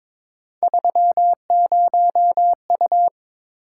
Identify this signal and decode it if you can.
Morse code, heard as a keyed tone, about 21 wpm, 700 Hz. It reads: 30U
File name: practice / 22wpm / Three Letter-Number Combinations - 22wpm